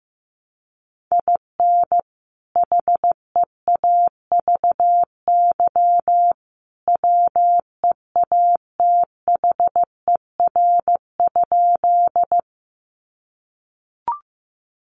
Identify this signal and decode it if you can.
Morse code, heard as a keyed tone, about 15 wpm, 700 Hz. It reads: IN HEAVY WEATHER?
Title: Sentences from Top 1100 Words 15wpm